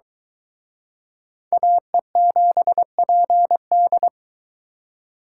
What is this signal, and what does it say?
Morse code, heard as a keyed tone, about 24 wpm, 700 Hz. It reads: AE7PD